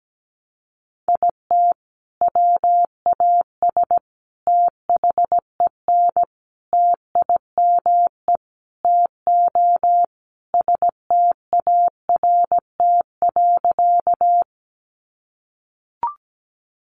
Morse code, heard as a keyed tone, about 17 wpm, 700 Hz. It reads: IT WAS THEN TIME TO START.
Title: Sentences from Top 300 Words 17wpm